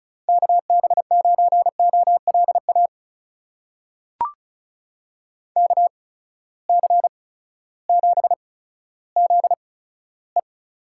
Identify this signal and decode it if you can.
Morse code, heard as a keyed tone, about 36 wpm, spaced as for 16 wpm, 700 Hz. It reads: KB9OLA KC7ZE
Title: USA Call Signs - 6 Characters - 8x Character Spacing - 35wpm